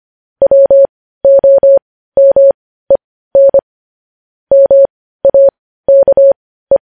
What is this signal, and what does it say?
Morse code, heard as a keyed tone, about 25 wpm, spaced as for 15 wpm, 550 Hz. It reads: WOMEN MAKE
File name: podcast / News headlines-1x-25x15